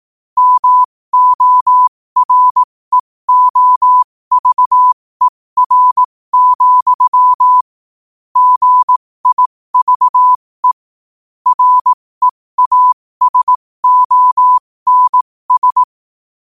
Morse code, heard as a keyed tone, about 18 wpm, spaced as for 15 wpm, 1000 Hz. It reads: MOREOVER, GIVE REASONS
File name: qotdfiles / 15wpm